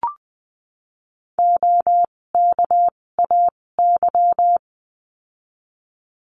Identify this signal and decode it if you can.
Morse code, heard as a keyed tone, about 20 wpm, spaced as for 16 wpm, 700 Hz. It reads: OKAY
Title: 256 Lesson 42a - Words 12-20wpm